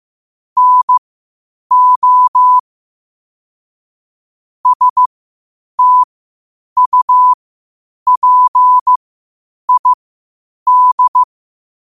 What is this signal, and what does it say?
Morse code, heard as a keyed tone, about 15 wpm, spaced as for 8 wpm, 1000 Hz. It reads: NO STUPID
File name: qotdfiles / 8wpm